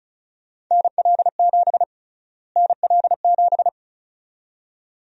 Morse code, heard as a keyed tone, about 36 wpm, 700 Hz. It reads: NL7 NL7
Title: USA Call Sign Prefixes - Repeated 2x - 35wpm